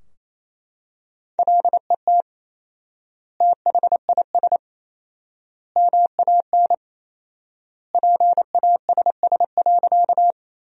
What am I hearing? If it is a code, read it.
Morse code, heard as a keyed tone, about 27 wpm, 700 Hz. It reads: LET THIS MAN PASS.
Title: 4x Word-Spacing - Sentences from Top 400 Words - 28wpm